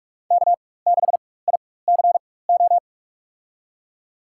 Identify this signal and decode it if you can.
Morse code, heard as a keyed tone, about 46 wpm, spaced as for 21 wpm, 700 Hz. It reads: K6ICO
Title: USA Call Signs - 5 Characters - 4x Character Spacing - 45wpm